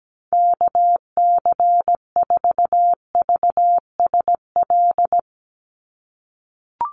Morse code, heard as a keyed tone, about 17 wpm, 700 Hz. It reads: KC4VSL
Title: USA Call Signs 17wpm